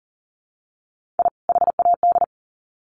Morse code, heard as a keyed tone, about 40 wpm, 700 Hz. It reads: I HAD
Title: Sets of 2 Words 40wpm